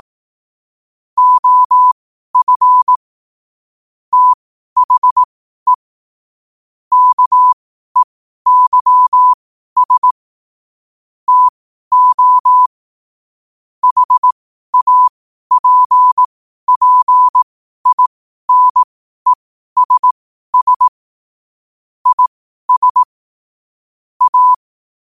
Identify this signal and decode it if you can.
Morse code, heard as a keyed tone, about 18 wpm, spaced as for 12 wpm, 1000 Hz. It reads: OF THE KEYS TO HAPPINESS IS A
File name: qotdfiles / 12wpm